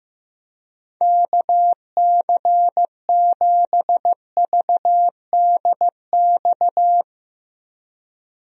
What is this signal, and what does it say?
Morse code, heard as a keyed tone, about 15 wpm, 700 Hz. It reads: KC7VDX
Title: USA Call Signs - 6 Characters - Repeated 3x - 15wpm